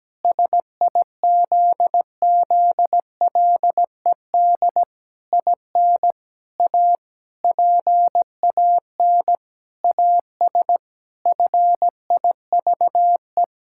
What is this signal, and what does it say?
Morse code, heard as a keyed tone, about 17 wpm, 700 Hz. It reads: SIZZLED IN A PAN AS FIVE
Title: Pangram Challenge v2 - 17wpm